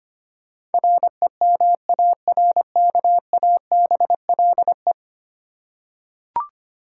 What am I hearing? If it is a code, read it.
Morse code, heard as a keyed tone, about 25 wpm, 700 Hz. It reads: REMARKABLE
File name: Ten Letter Words - 25wpm